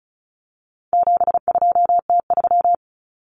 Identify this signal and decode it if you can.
Morse code, heard as a keyed tone, about 34 wpm, 700 Hz. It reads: 72T3
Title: Common Cut Numbers - 4-Digit 35wpm